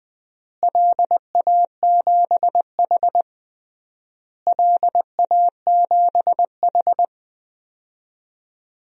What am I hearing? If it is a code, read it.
Morse code, heard as a keyed tone, about 20 wpm, 700 Hz. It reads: LA7H LA7H
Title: DXCC-100 Call Signs - 4 Characters - Repeated 3x - 20wpm